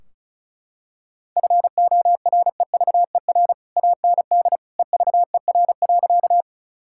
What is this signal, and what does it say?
Morse code, heard as a keyed tone, about 36 wpm, 700 Hz. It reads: FOREVER AND EVER.
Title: Sentences from Top 1600 Words 35wpm